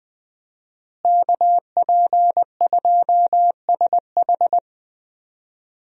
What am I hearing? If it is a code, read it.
Morse code, heard as a keyed tone, about 21 wpm, 700 Hz. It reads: KP2SH
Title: DXCC-100 Call Signs - 5 Characters - Repeated 2x - 20wpm